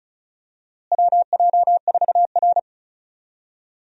Morse code, heard as a keyed tone, about 36 wpm, 700 Hz. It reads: WJ4R
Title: USA Call Signs - 4 Characters - Repeated 3x - 35wpm